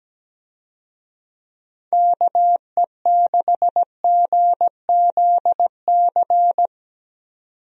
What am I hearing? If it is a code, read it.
Morse code, heard as a keyed tone, about 17 wpm, 700 Hz. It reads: KE6GZC
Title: USA Call Signs - 6 Characters - 17wpm